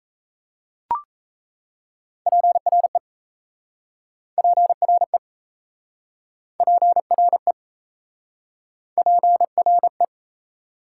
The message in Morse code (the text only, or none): PRE PRE PRE PRE